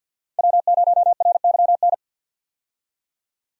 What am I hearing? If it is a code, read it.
Morse code, heard as a keyed tone, about 50 wpm, 700 Hz. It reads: W0RYN
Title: USA Call Signs - 5 Characters - 50wpm